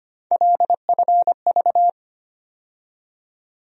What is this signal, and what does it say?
Morse code, heard as a keyed tone, about 25 wpm, 700 Hz. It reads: LFV
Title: USA Call Sign Postfixes - 3 Letters 25wpm